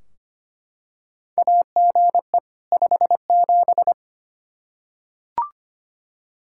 Morse code, heard as a keyed tone, about 25 wpm, 700 Hz. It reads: AGE 57
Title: Practice QSO - Exchange 3 - More Info - 25wpm